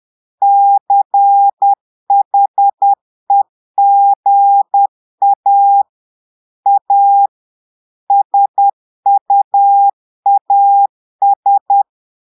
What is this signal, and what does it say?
Morse code, heard as a keyed tone, about 10 wpm, 800 Hz. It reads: CHEGA A SUAS
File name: Portugais - texte en clair 1 - vitesse - 10 mots minute